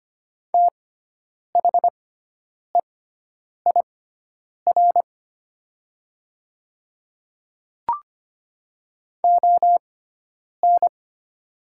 Morse code, heard as a keyed tone, about 25 wpm, spaced as for 9 wpm, 700 Hz. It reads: THEIR ON
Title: 6x Character Spacing - Top 100 Words - 25wpm